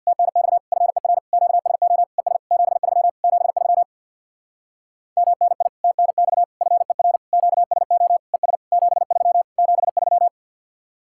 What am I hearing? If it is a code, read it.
Morse code, heard as a keyed tone, about 64 wpm, 700 Hz. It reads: TNX FER QSO ES 73 73 MNI TNX FER QSO ES 73 73